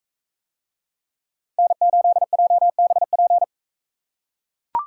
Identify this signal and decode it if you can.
Morse code, heard as a keyed tone, about 38 wpm, 700 Hz. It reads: N8JBP